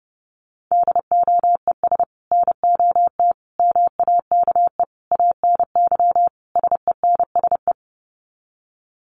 Morse code, heard as a keyed tone, about 30 wpm, 700 Hz. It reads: DOES NOT MAKE ANY SENSE
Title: Sets of 5 Words Encore2 30wpm